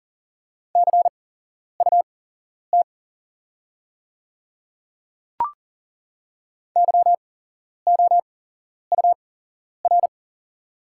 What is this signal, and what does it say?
Morse code, heard as a keyed tone, about 40 wpm, spaced as for 11 wpm, 690 Hz. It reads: CUT YOUR
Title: 8x Character Spacing - Top 300 Words - 40wpm